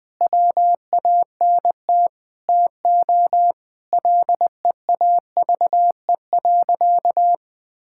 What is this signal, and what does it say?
Morse code, heard as a keyed tone, about 21 wpm, 700 Hz. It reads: WANT TO LEAVE.
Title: Sentences with 4-Word Phrases - 20wpm